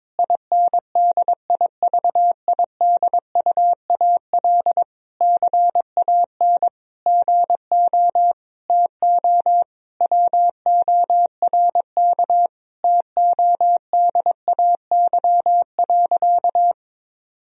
Morse code, heard as a keyed tone, about 22 wpm, 700 Hz. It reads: INDIVIDUAL CAN GO TO WORK TODAY.
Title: Sentences from Top 1000 Words - Review Part 4 - 22wpm